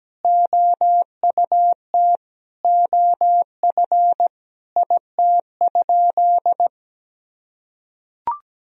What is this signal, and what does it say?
Morse code, heard as a keyed tone, about 17 wpm, 700 Hz. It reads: OUT OF IT?
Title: Sentences from Top 1000 Words - Review Part 4 - 17wpm